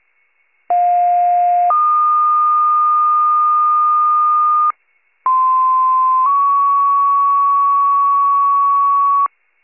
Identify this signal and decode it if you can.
Two-tone page, heard as a one-second tone, then a three-second tone, about 710 then 1220 Hz, about 1040 then 1120 Hz. It is heard from 0.7 to 9.3 s.
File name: pager / 46 RESCUE / wcalert